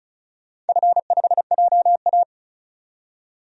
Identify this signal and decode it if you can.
Morse code, heard as a keyed tone, about 36 wpm, 700 Hz. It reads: F5JA
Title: DXCC-100 Call Signs - 4 Characters - 35wpm